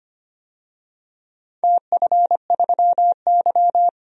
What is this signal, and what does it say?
Morse code, heard as a keyed tone, about 25 wpm, 700 Hz. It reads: TF3Y
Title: DXCC-100 Call Signs - 4 Characters - 25wpm